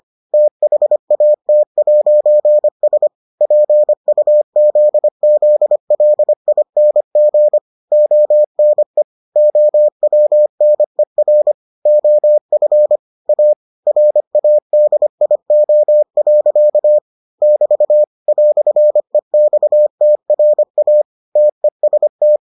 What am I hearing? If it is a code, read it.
Morse code, heard as a keyed tone, about 25 wpm, 600 Hz. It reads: THAT'S PUZZLING ONE OWNER OF A RADIO. <BT> "EXTRA TEST